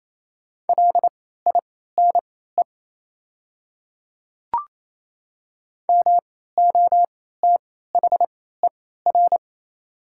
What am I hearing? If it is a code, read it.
Morse code, heard as a keyed tone, about 28 wpm, spaced as for 16 wpm, 690 Hz. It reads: LINE MOTHER
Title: 3x Character Spacing - Top 200 Words - 28wpm